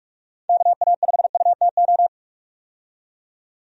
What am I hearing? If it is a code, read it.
Morse code, heard as a keyed tone, about 44 wpm, 700 Hz. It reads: KA5UTO
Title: USA Call Signs - 6 Characters - 45wpm